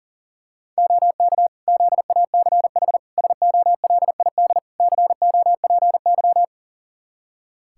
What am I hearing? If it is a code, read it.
Morse code, heard as a keyed tone, about 37 wpm, 700 Hz. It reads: OK ZACH SOLID COPY
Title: K1USN SST QSO Practice 40wpm